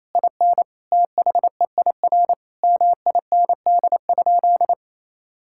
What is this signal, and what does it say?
Morse code, heard as a keyed tone, about 28 wpm, 700 Hz. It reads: IN THEIR MIND?
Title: Sentences from Top 1000 Words - Review Part 5 - 28wpm